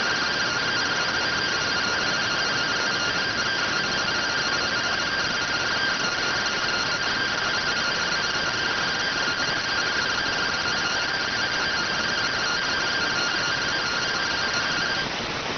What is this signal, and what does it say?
Two-tone teleprinter signal, about 170 Hz shift.